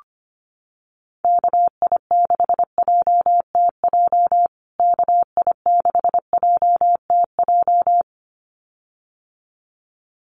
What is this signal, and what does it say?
Morse code, heard as a keyed tone, about 25 wpm, 700 Hz. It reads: KI6JTJ KI6JTJ
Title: USA Call Signs Repeated 2x 25wpm